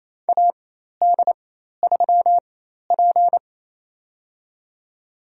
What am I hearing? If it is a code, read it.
Morse code, heard as a keyed tone, about 29 wpm, spaced as for 13 wpm, 700 Hz. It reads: AD3P
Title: USA Call Signs - 4 Characters - 4x Character Spacing - 28wpm